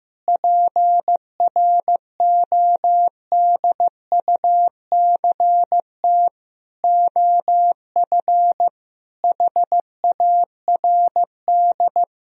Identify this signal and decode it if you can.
Morse code, heard as a keyed tone, about 15 wpm, 700 Hz. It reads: PRODUCT OF HARD
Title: Sentences from Top 1000 Words - Review Part 2 - 15wpm